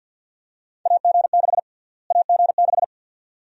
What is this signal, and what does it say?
Morse code, heard as a keyed tone, about 52 wpm, 700 Hz. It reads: AG6 AG6
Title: USA Call Sign Prefixes - Repeated 2x - 50wpm